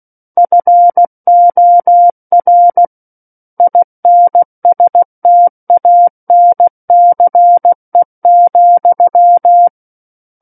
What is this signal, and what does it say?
Morse code, heard as a keyed tone, about 16 wpm, 700 Hz. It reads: FOR INSTANCE,
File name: Book One / 07-Chapter 7 - 16 WPM